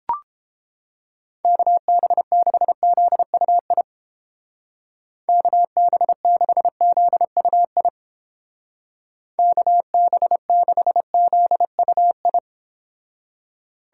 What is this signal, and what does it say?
Morse code, heard as a keyed tone, about 30 wpm, 700 Hz. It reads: KB6ZUI KB6ZUI KB6ZUI